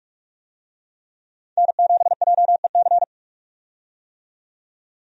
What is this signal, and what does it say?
Morse code, heard as a keyed tone, about 41 wpm, 700 Hz. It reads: N7JEC
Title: USA Call Signs - 5 Characters - 45wpm